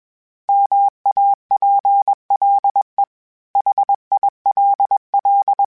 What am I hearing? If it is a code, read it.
Morse code, heard as a keyed tone, about 21 wpm, 800 Hz. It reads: MAPLE HILL
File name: fromSMs / FromSMs mp3s / Cities@20